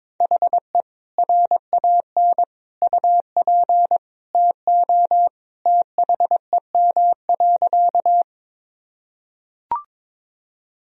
Morse code, heard as a keyed tone, about 22 wpm, 700 Hz. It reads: HE RAN UP TO THEM.
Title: Sentences from Top 1000 Words - Review Part 3 - 22wpm